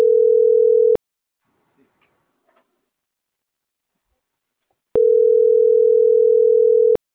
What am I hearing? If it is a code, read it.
Ringback tone (North America), 440 + 480 Hz.